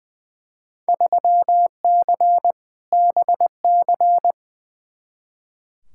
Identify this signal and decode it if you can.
Morse code, heard as a keyed tone, about 20 wpm, 700 Hz. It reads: 3C BC
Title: Field Day Exchanges Encore - 20wpm